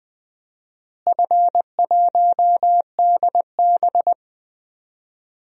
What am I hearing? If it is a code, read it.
Morse code, heard as a keyed tone, about 20 wpm, 700 Hz. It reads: F1DB
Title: DXCC-100 Call Signs - 4 Characters - Repeated 2x - 20wpm